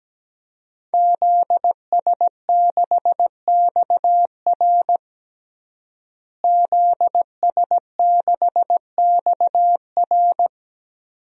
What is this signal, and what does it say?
Morse code, heard as a keyed tone, about 17 wpm, 700 Hz. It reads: ZS6XR ZS6XR
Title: DXCC-100 Call Signs - 5 Characters - Repeated 3x - 17wpm